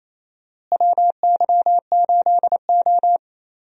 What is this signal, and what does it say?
Morse code, heard as a keyed tone, about 28 wpm, 700 Hz. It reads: WY8O
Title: USA Call Signs - 4 Characters - Repeated 3x - 28wpm